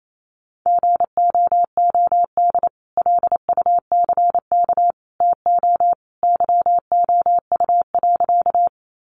Morse code, heard as a keyed tone, about 28 wpm, 700 Hz. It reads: GOOD LUCK TO YOU.
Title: Sentences from Top 1500 Words 28wpm